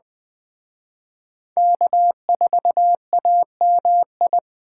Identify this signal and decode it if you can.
Morse code, heard as a keyed tone, about 21 wpm, 700 Hz. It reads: K4AMI
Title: Speed Racing - USA 1x3 Call Signs 17wpm